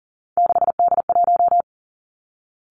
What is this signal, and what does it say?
Morse code, heard as a keyed tone, about 38 wpm, 700 Hz. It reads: 6D1
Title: Three Challenging Letter-Number Combinations 40wpm